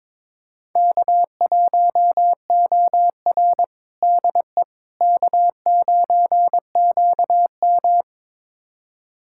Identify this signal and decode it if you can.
Morse code, heard as a keyed tone, about 23 wpm, 700 Hz. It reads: K1OR DE K9QM
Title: USA Callsign DE Callsign - 4 Characters - 22wpm